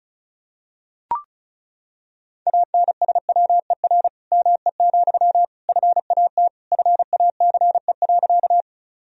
Morse code, heard as a keyed tone, about 36 wpm, 700 Hz. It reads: ANSWER ME, FAT FACE.
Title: Sentences from Top 1000 Words - Review Part 4 - 35wpm